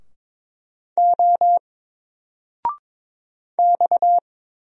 Morse code, heard as a keyed tone, about 22 wpm, 700 Hz. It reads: O X
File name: Single-Letter-Number-22wpm